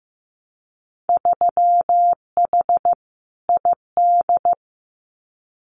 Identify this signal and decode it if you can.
Morse code, heard as a keyed tone, about 15 wpm, 700 Hz. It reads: 3H ID
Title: Winter Field Day Exchanges Encore 15wpm